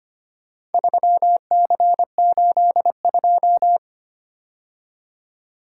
Morse code, heard as a keyed tone, about 25 wpm, 700 Hz. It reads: 3C82
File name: practice / 25wpm / Four Letter-Number Combinations - 25wpm